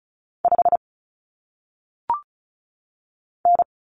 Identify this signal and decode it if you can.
Morse code, heard as a keyed tone, about 36 wpm, 700 Hz. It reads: 5 N
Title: Common Cut Numbers - 1-Digit 35wpm